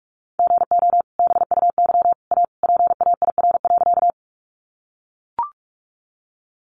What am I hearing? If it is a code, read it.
Morse code, heard as a keyed tone, about 44 wpm, 700 Hz. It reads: GO BUY A PAIR.
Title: Sentences from Top 1200 Words 45wpm